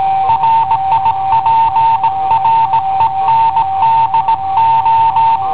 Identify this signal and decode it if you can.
Morse code, heard as a keyed tone, about 16 wpm, 900 Hz. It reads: RIPRENDO
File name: i7ohp / cassetta i7ohp B